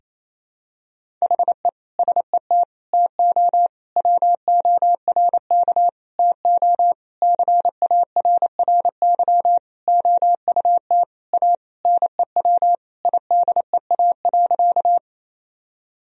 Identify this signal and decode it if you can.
Morse code, heard as a keyed tone, about 28 wpm, 700 Hz. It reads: HE SET TO WORK TO CARRY OUT A NEW IDEA.
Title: Sentences from Top 1000 Words - Review Part 1 - 28wpm